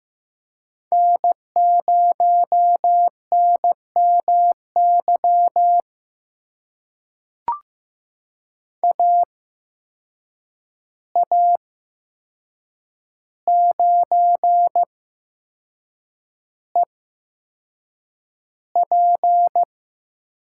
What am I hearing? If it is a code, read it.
Morse code, heard as a keyed tone, about 15 wpm, spaced as for 6 wpm, 700 Hz. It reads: N0NMY AA9EP
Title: USA Call Signs - 5 Characters - 8x Character Spacing - 15wpm